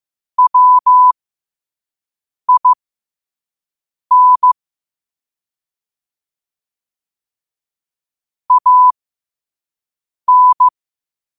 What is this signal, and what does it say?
Morse code, heard as a keyed tone, about 15 wpm, spaced as for 5 wpm, 1000 Hz. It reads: WIN AN
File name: qotdfiles / 5wpm